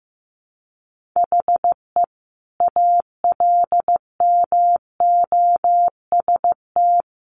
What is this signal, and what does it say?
Morse code, heard as a keyed tone, about 15 wpm, 700 Hz. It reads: HE ALMOST
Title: Sentences from Top 1400 Words 15wpm